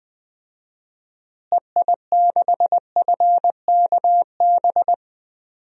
Morse code, heard as a keyed tone, about 20 wpm, 700 Hz. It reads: EI6FKB
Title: DXCC-100 Call Signs - 6 Characters - Repeated 2x - 20wpm